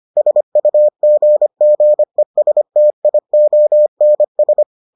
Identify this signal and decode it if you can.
Morse code, heard as a keyed tone, about 25 wpm, 600 Hz. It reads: SUGGESTIONS